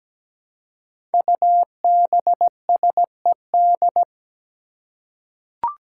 Morse code, heard as a keyed tone, about 17 wpm, 700 Hz. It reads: UBSED
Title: Five Challenging Letter-Number Combinations 17wpm